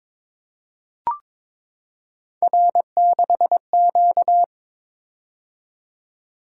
Morse code, heard as a keyed tone, about 22 wpm, 700 Hz. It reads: R6Q
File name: practice / 22wpm / Three Letter-Number Combinations - 22wpm